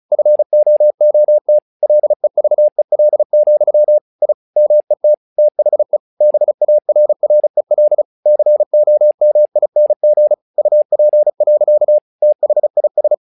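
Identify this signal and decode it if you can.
Morse code, heard as a keyed tone, about 36 wpm, 600 Hz. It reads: FOOT LEVEL, I MET THE BARREL COMING UP. THIS
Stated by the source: Morse practice text "Tower Trouble" sent at 35 WPM, from the SKCC - Straight Key Century Club learning center audio files.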